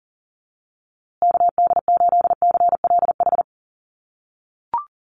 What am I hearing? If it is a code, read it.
Morse code, heard as a keyed tone, about 40 wpm, 700 Hz. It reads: KD8CLH
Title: USA Call Signs Repeated 2x 40wpm